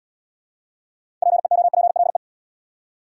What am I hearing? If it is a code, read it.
Morse code, heard as a keyed tone, about 72 wpm, 700 Hz. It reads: PEOPLE